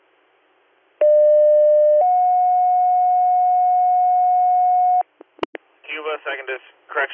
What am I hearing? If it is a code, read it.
Two-tone page, heard as a one-second tone, then a three-second tone, about 600 then 730 Hz, with speech. It is heard from 1.0 to 5.0 s.